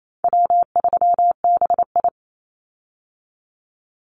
Morse code, heard as a keyed tone, about 28 wpm, 700 Hz. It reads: W3BI
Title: Call Signs Repeated 2x - 28wpm